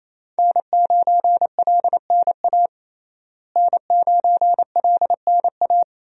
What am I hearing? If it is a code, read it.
Morse code, heard as a keyed tone, about 28 wpm, 700 Hz. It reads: N9LNA N9LNA
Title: USA Call Signs - 5 Characters - Repeated 2x - 28wpm